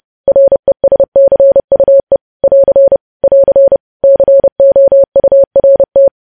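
Morse code, heard as a keyed tone, about 30 wpm, 550 Hz. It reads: RESCUE <AR> <AR> COURT
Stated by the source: News headlines sent in Morse at 30 wpm, preceded by VVV.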